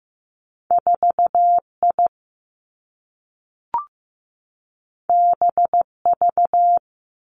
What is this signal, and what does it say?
Morse code, heard as a keyed tone, about 15 wpm, 700 Hz. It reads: 4 I B V
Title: Two Challenging Letter-Number Combinations 15wpm